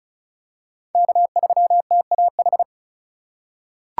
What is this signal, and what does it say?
Morse code, heard as a keyed tone, about 34 wpm, 700 Hz. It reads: K3TAH
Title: USA Call Signs - 2x Character Spacing - 35wpm